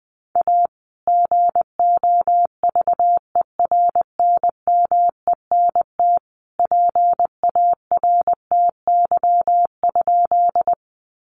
Morse code, heard as a keyed tone, about 19 wpm, 700 Hz. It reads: A GOVERNMENT PARTY?
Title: Sentences from Top 1400 Words 20wpm